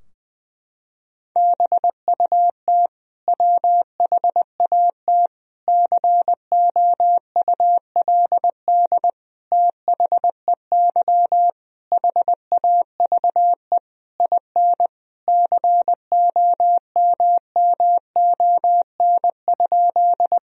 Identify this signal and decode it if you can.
Morse code, heard as a keyed tone, about 20 wpm, 700 Hz. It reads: BUT WHAT COULD THEY HAVE IN COMMON?
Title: Sentences from Top 1000 Words - Review Part 3 - 20wpm